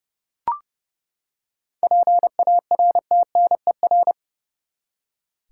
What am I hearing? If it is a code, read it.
Morse code, heard as a keyed tone, about 32 wpm, 700 Hz. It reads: PARTNER
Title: Seven Letter Words - 30wpm